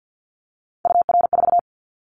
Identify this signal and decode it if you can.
Morse code, heard as a keyed tone, about 50 wpm, 700 Hz. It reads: UR4
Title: Callsign Prefixes for Top 25 DXCC - 50wpm